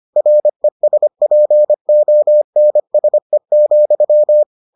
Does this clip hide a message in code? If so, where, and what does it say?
Yes: Morse code from 0.2 to 4.4 s, heard as a keyed tone, about 25 wpm, 600 Hz. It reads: RESPONSE,